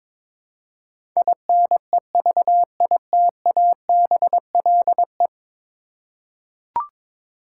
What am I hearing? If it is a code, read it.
Morse code, heard as a keyed tone, about 21 wpm, 700 Hz. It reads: INEVITABLE
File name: practice / 22wpm / 10 Letter Words - First 3 Letters Given - 22wpm